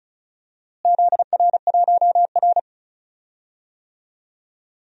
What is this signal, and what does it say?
Morse code, heard as a keyed tone, about 36 wpm, 700 Hz. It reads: ZR1R
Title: DXCC-100 Call Signs - 4 Characters - Repeated 3x - 35wpm